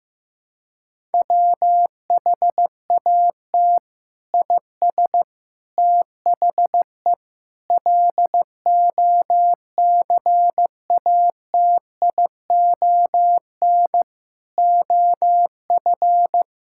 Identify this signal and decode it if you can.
Morse code, heard as a keyed tone, about 15 wpm, 700 Hz. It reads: WHAT IS THE LOCATION OF
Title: Sentences from Top 1000 Words - Review Part 6 - 15wpm